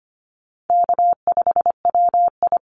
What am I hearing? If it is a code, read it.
Morse code, heard as a keyed tone, about 25 wpm, 700 Hz. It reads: K5WI
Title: NA SOTA Chaser Callsigns 25wpm